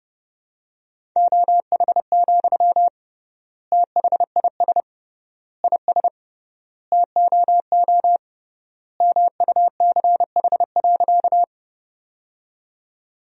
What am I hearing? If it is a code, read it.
Morse code, heard as a keyed tone, about 30 wpm, 700 Hz. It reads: OH, THIS IS TOO MUCH.
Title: 3x Word-Spacing - Sentences from Top 500 Words - 30wpm